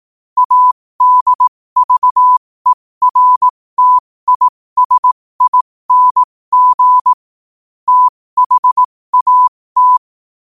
Morse code, heard as a keyed tone, about 18 wpm, spaced as for 15 wpm, 1000 Hz. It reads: ADVERTISING THAT
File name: qotdfiles / 15wpm